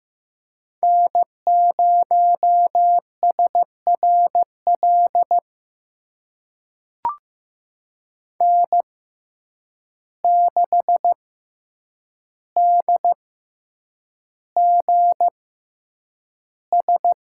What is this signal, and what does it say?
Morse code, heard as a keyed tone, about 15 wpm, spaced as for 8 wpm, 700 Hz. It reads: N0SRL N6DGS
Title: USA Call Signs - 5 Characters - 6x Character Spacing - 15wpm